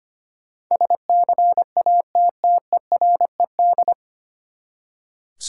Morse code, heard as a keyed tone, about 25 wpm, 700 Hz. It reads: SCATTERED
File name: Nine Letter Words - 25wpm